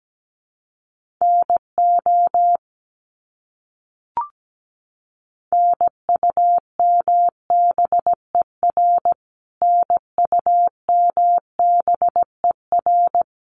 Morse code, heard as a keyed tone, about 17 wpm, 700 Hz. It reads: NO NUMBER NUMBER
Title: Top 300 Words Repeated 17wpm